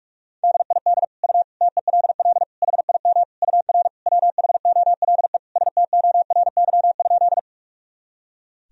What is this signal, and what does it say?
Morse code, heard as a keyed tone, about 41 wpm, 700 Hz. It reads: DID U TELL HIM UR WHOLE STORY?
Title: Sentences from Top 400 Words and Abbreviations 45wpm